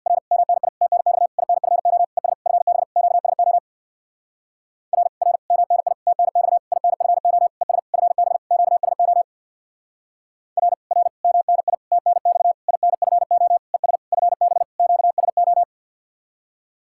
Morse code, heard as a keyed tone, about 66 wpm, 700 Hz. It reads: R MNI TNX INFO ES FB QSO R R MNI TNX INFO ES FB QSO R R MNI TNX INFO ES FB QSO